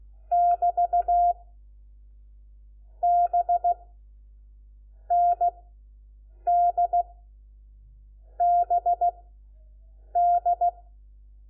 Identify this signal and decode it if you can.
Morse code, heard as a keyed tone, about 16 wpm, spaced as for 7 wpm, 700 Hz. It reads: <BT> B ND BD
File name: Lecciones / CLASE Nº4